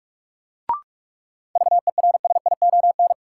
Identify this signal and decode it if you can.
Morse code, heard as a keyed tone, about 43 wpm, 700 Hz. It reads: VERSION
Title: Sentences from Top 1400 Words - New Word Drill - 45wpm